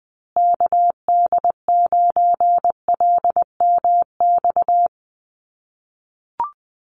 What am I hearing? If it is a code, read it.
Morse code, heard as a keyed tone, about 20 wpm, 700 Hz. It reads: KD9LMX
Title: USA Call Signs Repeated 3x 20wpm